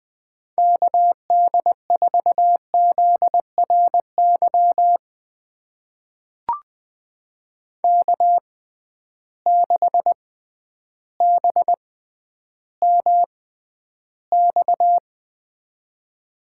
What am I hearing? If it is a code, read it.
Morse code, heard as a keyed tone, about 20 wpm, spaced as for 11 wpm, 700 Hz. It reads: KD4ZRY K6BMX
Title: USA Call Signs - 6x Character Spacing - 20wpm